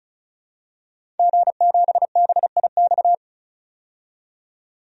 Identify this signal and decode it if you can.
Morse code, heard as a keyed tone, about 36 wpm, 700 Hz. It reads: G7BIX
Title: DXCC-100 Call Signs - 5 Characters - 35wpm